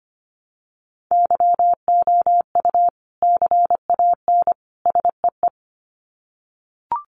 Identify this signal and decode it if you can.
Morse code, heard as a keyed tone, about 25 wpm, 700 Hz. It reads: YOU CAN SEE
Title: Sets of 3 Words 25wpm